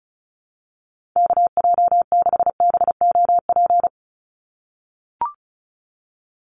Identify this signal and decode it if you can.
Morse code, heard as a keyed tone, about 34 wpm, 700 Hz. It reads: KJ6BOP
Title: USA Call Signs 35wpm